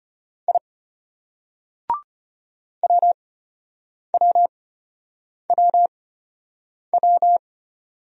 Morse code, heard as a keyed tone, about 32 wpm, spaced as for 8 wpm, 700 Hz. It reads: I WWWW